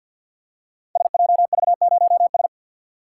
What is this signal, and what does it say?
Morse code, heard as a keyed tone, about 46 wpm, 700 Hz. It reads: SJV0S
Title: Code Groups - 5 Letter:Numbers - 50wpm